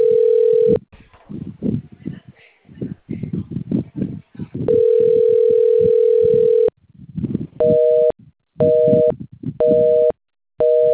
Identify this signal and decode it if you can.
Ringback tone (North America), 440 + 480 Hz.